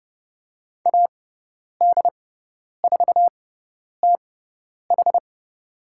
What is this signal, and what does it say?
Morse code, heard as a keyed tone, about 30 wpm, spaced as for 10 wpm, 700 Hz. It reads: AD4TH
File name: Lesson 18g - Letter m - Call Signs 30wpm